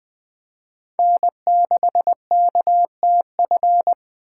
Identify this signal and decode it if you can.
Morse code, heard as a keyed tone, about 20 wpm, 700 Hz. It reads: N6KTF
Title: USA Call Signs - 5 Characters - 2x Character Spacing - 20wpm